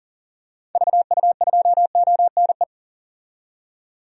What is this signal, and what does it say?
Morse code, heard as a keyed tone, about 37 wpm, 700 Hz. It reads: VU2ONE
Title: DXCC-100 Call Signs - 6 Characters - 40wpm